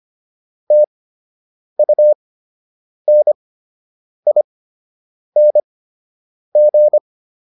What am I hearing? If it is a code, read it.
Morse code, heard as a keyed tone, about 25 wpm, spaced as for 8 wpm, 600 Hz. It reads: TUNING